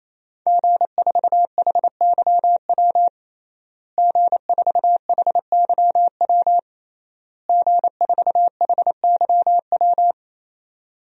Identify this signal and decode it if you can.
Morse code, heard as a keyed tone, about 28 wpm, 700 Hz. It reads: G4HYW G4HYW G4HYW
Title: DXCC-100 Call Signs - 5 Characters - Repeated 3x - 28wpm